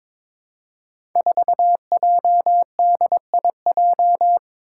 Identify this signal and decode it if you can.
Morse code, heard as a keyed tone, about 23 wpm, 700 Hz. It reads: 4JDIJ
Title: Five Challenging Letter-Number Combinations 22wpm